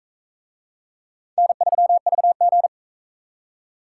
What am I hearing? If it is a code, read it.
Morse code, heard as a keyed tone, about 42 wpm, 700 Hz. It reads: N3VG